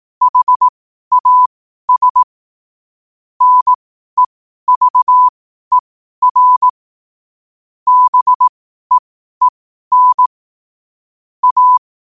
Morse code, heard as a keyed tone, about 18 wpm, spaced as for 12 wpm, 1000 Hz. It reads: HAS NEVER BEEN A
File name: qotdfiles / 12wpm